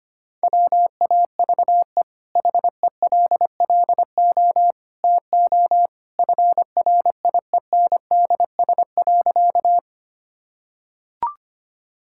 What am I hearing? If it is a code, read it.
Morse code, heard as a keyed tone, about 25 wpm, 700 Hz. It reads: WAVE HELLO TO FRIENDS.
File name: Easy Sentences with Hints - 25wpm